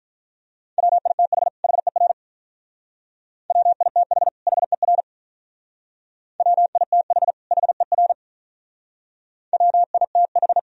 Morse code, heard as a keyed tone, about 43 wpm, 700 Hz. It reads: WITH HER WITH HER WITH HER WITH
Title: Speed-Racing - Sets of 2 Words 35wpm